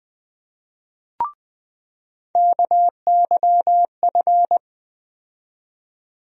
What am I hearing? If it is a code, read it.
Morse code, heard as a keyed tone, about 20 wpm, 700 Hz. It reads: KYF
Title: USA Call Sign Postfixes - 3 Letters 20wpm